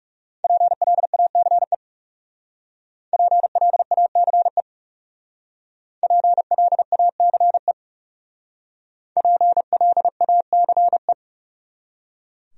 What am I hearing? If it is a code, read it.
Morse code, heard as a keyed tone, about 38 wpm, 700 Hz. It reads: PLACE PLACE PLACE PLACE